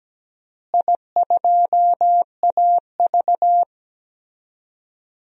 Morse code, heard as a keyed tone, about 17 wpm, 700 Hz. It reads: I2AV
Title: DXCC-100 Call Signs - 4 Characters - Repeated 2x - 17wpm